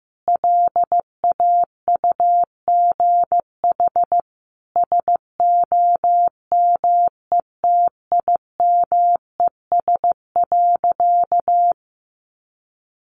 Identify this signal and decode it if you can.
Morse code, heard as a keyed tone, about 14 wpm, 700 Hz. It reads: LAUGH SOMETIMES.
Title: Sentences from Top 600 Words 15wpm